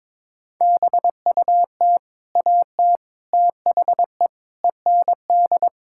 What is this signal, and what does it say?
Morse code, heard as a keyed tone, about 22 wpm, 700 Hz. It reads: BUT AT THE END
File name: practice / 22wpm / Sentences with 8-Word Phrases - 22wpm